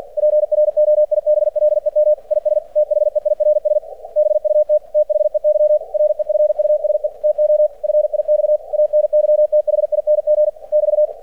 Morse code, heard as a keyed tone, about 50 wpm, 600 Hz. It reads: PROIBLEM IS THEIRS BUT THEY REFUSE TO FIX ANYTHING <BT>
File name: archived / CRSnet